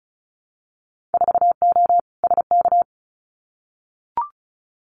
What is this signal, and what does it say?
Morse code, heard as a keyed tone, about 36 wpm, 700 Hz. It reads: 4O SK
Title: Winter Field Day Exchanges Encore 35wpm